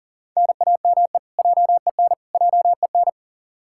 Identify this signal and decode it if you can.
Morse code, heard as a keyed tone, about 40 wpm, 700 Hz. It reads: NAME JEN JEN
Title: K1USN SST QSO Practice 40wpm